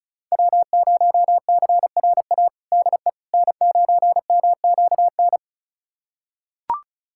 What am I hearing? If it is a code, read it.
Morse code, heard as a keyed tone, about 36 wpm, 690 Hz. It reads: W0CRA DE N9MQN
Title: USA Callsign DE Callsign - 5 Characters - 35wpm